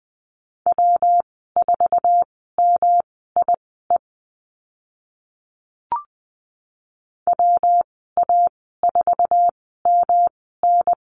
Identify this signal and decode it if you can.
Morse code, heard as a keyed tone, about 20 wpm, spaced as for 14 wpm, 700 Hz. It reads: W4MIE WA4MN
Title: 122 Lesson 20g - Number 3 - Call Signs 20wpm